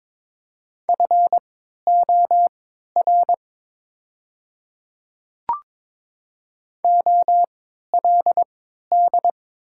Morse code, heard as a keyed tone, about 22 wpm, spaced as for 12 wpm, 690 Hz. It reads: FOR OLD
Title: 3x Character Spacing - Top 200 Words - 22wpm